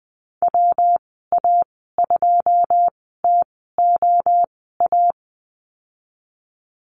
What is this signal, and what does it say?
Morse code, heard as a keyed tone, about 20 wpm, spaced as for 14 wpm, 700 Hz. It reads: WA2TOA
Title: 128 Lesson 21g - Number 6 - Call Signs 20wpm